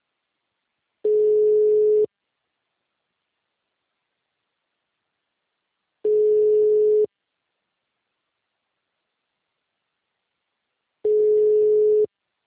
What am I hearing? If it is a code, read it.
Ringback tone (Europe, 425 Hz).